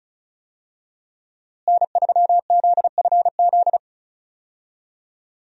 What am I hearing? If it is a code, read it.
Morse code, heard as a keyed tone, about 34 wpm, 700 Hz. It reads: N3ZFZ
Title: USA Call Signs - 5 Characters - Repeated 2x - 35wpm